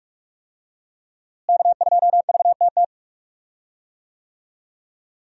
Morse code, heard as a keyed tone, about 43 wpm, 700 Hz. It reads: K2VTT
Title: USA Call Signs - 5 Characters - Repeated 2x - 45wpm